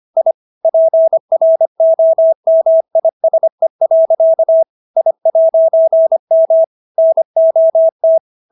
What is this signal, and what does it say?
Morse code, heard as a keyed tone, about 25 wpm, 650 Hz. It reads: I PROMISE. I'M NOT